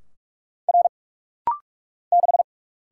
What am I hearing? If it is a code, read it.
Morse code, heard as a keyed tone, about 46 wpm, 700 Hz. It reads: R 6
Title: Single Letter-Number Alt - Rapid-Fire 45wpm